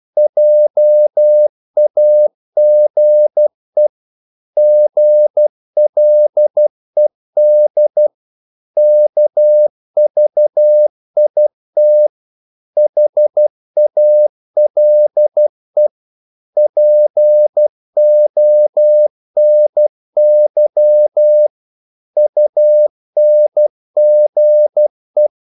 Morse code, heard as a keyed tone, about 12 wpm, 600 Hz. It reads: JAGE GLED KVIT HALE PONY UNGE
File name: Korte ord DK 12wpm